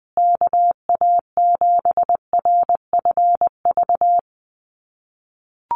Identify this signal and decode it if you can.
Morse code, heard as a keyed tone, about 20 wpm, 700 Hz. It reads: KA7RFV
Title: USA Call Signs Repeated 2x 20wpm